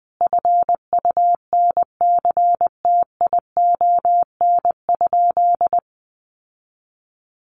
Morse code, heard as a keyed tone, about 20 wpm, 700 Hz. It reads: FUNCTION?
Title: Sentences from Top 1100 Words 20wpm